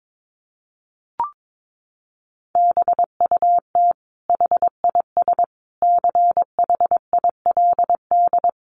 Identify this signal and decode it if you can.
Morse code, heard as a keyed tone, about 22 wpm, 700 Hz. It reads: BUT HIS CHILD
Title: Sentences from Top 800 Words 22wpm